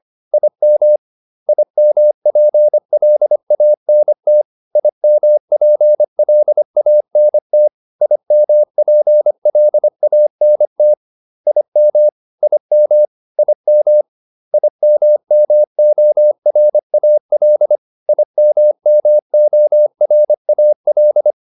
Morse code, heard as a keyed tone, about 25 wpm, 600 Hz. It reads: IM IMPLANT IMPLANT IMPLANT IM IM IM IMMORAL IMMORAL